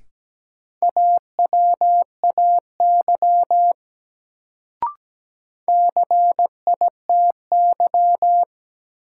Morse code, heard as a keyed tone, about 17 wpm, 700 Hz. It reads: AWAY CITY
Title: Sets of 2 Words - New Word Drill - 17wpm